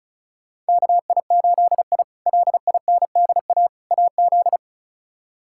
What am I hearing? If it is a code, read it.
Morse code, heard as a keyed tone, about 36 wpm, 700 Hz. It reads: KI8I LINDA AZ
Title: K1USN SST QSO Practice - 35wpm